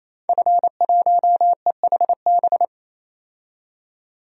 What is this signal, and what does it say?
Morse code, heard as a keyed tone, about 28 wpm, 700 Hz. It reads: F1EHB
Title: DXCC-100 Call Signs - 5 Characters - 28wpm